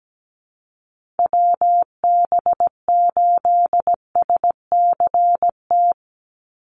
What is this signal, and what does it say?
Morse code, heard as a keyed tone, about 17 wpm, 700 Hz. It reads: WB8SCT
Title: Call Signs - 17wpm